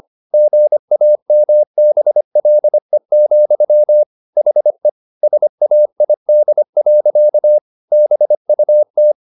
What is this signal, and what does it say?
Morse code, heard as a keyed tone, about 25 wpm, 600 Hz. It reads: GAMBLE, HE SAID. BUT
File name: the old man and the sea part 1 25wpm